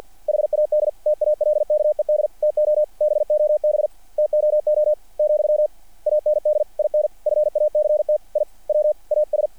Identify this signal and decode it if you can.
Morse code, heard as a keyed tone, about 50 wpm, 600 Hz. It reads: HAD TALKED TO BOB TOO , AND IN FACT I WAS
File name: archived / CRSnet